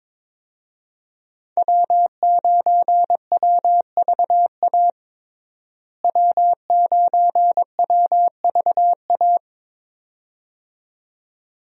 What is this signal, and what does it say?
Morse code, heard as a keyed tone, about 22 wpm, 700 Hz. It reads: W9WVA W9WVA
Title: USA Call Signs - 5 Characters - Repeated 2x - 22wpm